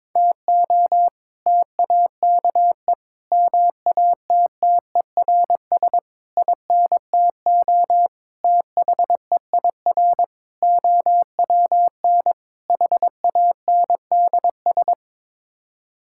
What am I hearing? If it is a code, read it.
Morse code, heard as a keyed tone, about 22 wpm, 700 Hz. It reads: TO TAKE MATTERS INTO THEIR OWN HANDS
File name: practice / 22wpm / Sets of 7 Words - 22wpm